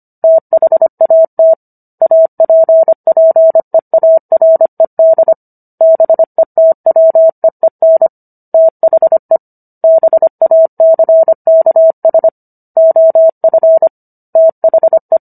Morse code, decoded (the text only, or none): THAT APPEARED BETWEEN THE BACKS OF THE